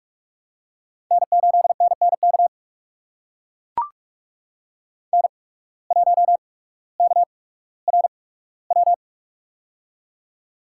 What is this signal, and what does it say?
Morse code, heard as a keyed tone, about 44 wpm, spaced as for 19 wpm, 690 Hz. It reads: N8NNK N1KRW
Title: USA Call Signs - 5 Characters - 8x Character Spacing - 45wpm